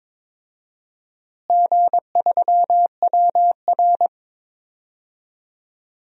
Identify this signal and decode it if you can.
Morse code, heard as a keyed tone, about 21 wpm, 700 Hz. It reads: G3WR
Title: DXCC-100 Call Signs - 4 Characters - Repeated 3x - 22wpm